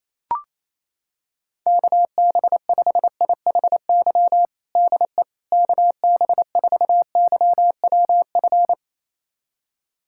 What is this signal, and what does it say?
Morse code, heard as a keyed tone, about 28 wpm, 700 Hz. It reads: KB5IHY DE KB4YWF
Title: USA Callsign DE Callsign - 6 Characters - 28wpm